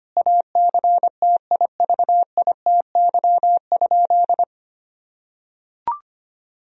Morse code, heard as a keyed tone, about 25 wpm, 700 Hz. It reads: ACTIVITY?
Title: Sentences from Top 1000 Words - Review Part 4 - 25wpm